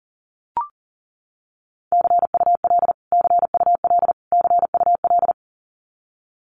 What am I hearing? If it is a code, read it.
Morse code, heard as a keyed tone, about 40 wpm, 700 Hz. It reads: CUL CUL CUL
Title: Abbreviations Repeated 3x - 40wpm